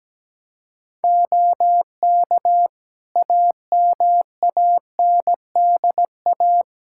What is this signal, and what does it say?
Morse code, heard as a keyed tone, about 17 wpm, 700 Hz. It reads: OK AMANDA
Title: K1USN SST QSO Practice 17wpm